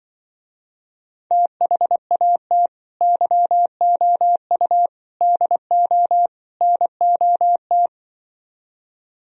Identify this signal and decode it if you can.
Morse code, heard as a keyed tone, about 24 wpm, 700 Hz. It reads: THAT YOU DO NOT
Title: Speed-Racing - Sets of 4 Words 20wpm